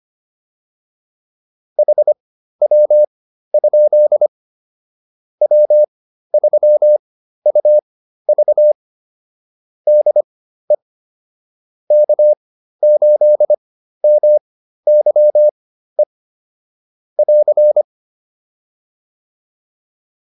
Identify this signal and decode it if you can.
Morse code, heard as a keyed tone, about 25 wpm, spaced as for 13 wpm, 600 Hz. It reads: HW? W3UV DE K8MYE <AR>